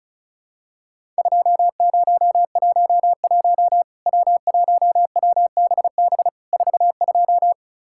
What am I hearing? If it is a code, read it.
Morse code, heard as a keyed tone, about 36 wpm, 700 Hz. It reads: 2011 W1WBB 42
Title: Worked All Europe DX Contest - QTC Exchange - 35wpm